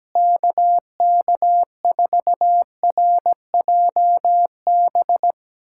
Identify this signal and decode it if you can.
Morse code, heard as a keyed tone, about 17 wpm, 700 Hz. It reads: KK4RJB
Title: USA Callsign DE Callsign - 6 Characters - 17wpm